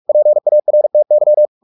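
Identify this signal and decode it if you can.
Morse code, heard as a keyed tone, about 44 wpm, 600 Hz. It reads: PARTY
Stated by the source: Morse practice at 45 wpm, the novel Never by Ken Follett in English, from CW med Gnister.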